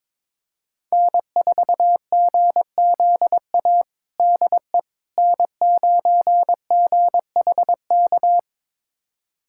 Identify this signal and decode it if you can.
Morse code, heard as a keyed tone, about 21 wpm, 700 Hz. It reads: N4GZA DE N9GHK
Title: USA Callsign DE Callsign - 5 Characters - 22wpm